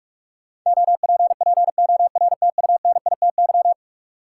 Morse code, heard as a keyed tone, about 42 wpm, 700 Hz. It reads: OPPORTUNITY
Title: Sentences from Top 1000 Words - New Word Drill - 45wpm